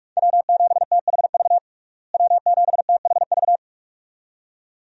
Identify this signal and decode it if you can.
Morse code, heard as a keyed tone, about 46 wpm, 700 Hz. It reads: W7THV W7THV
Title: USA Call Signs - 5 Characters - Repeated 2x - 45wpm